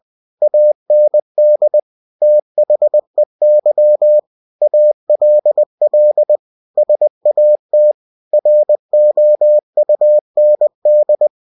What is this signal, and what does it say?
Morse code, heard as a keyed tone, about 20 wpm, 600 Hz. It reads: AND THEY ALL SAT ROUND
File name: cw-books / winnie-the-pooh / winnie-20wpm-page-0106